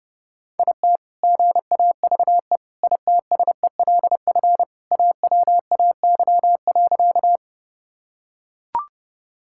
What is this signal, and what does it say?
Morse code, heard as a keyed tone, about 30 wpm, 700 Hz. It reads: IT GAVE ITSELF AWAY.
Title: Sentences from Top 1000 Words - Review Part 5 - 30wpm